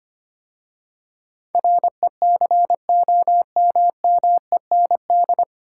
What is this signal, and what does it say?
Morse code, heard as a keyed tone, about 25 wpm, 700 Hz. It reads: RECOMMEND
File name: practice / 25wpm / 9 Letter Words - First 3 Letters Given - 25wpm